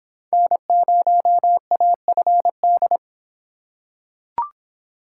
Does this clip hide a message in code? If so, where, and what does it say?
Yes: Morse code from 0.3 to 3.0 s, heard as a keyed tone, about 24 wpm, 700 Hz. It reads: N0AFD